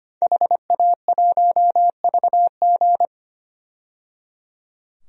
Morse code, heard as a keyed tone, about 25 wpm, 700 Hz. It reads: HA1VG
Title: DXCC-100 Call Signs - 5 Characters - 25wpm